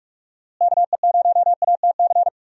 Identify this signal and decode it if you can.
Morse code, heard as a keyed tone, about 46 wpm, 700 Hz. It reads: KE0ATC
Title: USA Call Signs - 6 Characters - Repeated 3x - 45wpm